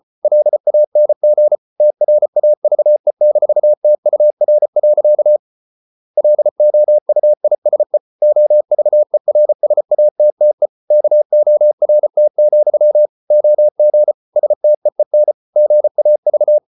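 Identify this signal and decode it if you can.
Morse code, heard as a keyed tone, about 34 wpm, 600 Hz. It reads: LANG TRAVE-TUR. LOUISE OVERSATTE KORT, OG STEEN GAV
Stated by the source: Morse code at 33 wpm, QRQ.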